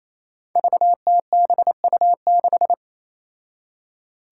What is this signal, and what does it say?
Morse code, heard as a keyed tone, about 28 wpm, 700 Hz. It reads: VTBU6
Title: Five Challenging Letter-Number Combinations 28wpm